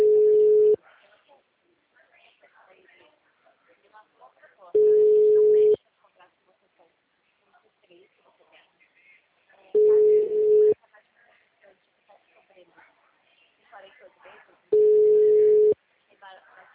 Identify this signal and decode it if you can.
Ringback tone (Europe, 425 Hz), with speech.